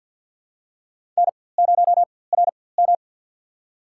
Morse code, heard as a keyed tone, about 53 wpm, spaced as for 24 wpm, 700 Hz. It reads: N0RM
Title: USA Call Signs - 4x Character Spacing - 50wpm